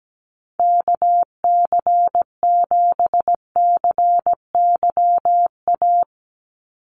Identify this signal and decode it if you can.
Morse code, heard as a keyed tone, about 17 wpm, 700 Hz. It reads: KC7CYA
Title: USA Call Signs 17wpm